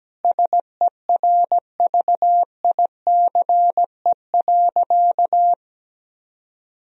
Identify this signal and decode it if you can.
Morse code, heard as a keyed tone, about 17 wpm, 700 Hz. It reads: SERVICE.
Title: Sentences from Top 1000 Words - Review Part 4 - 17wpm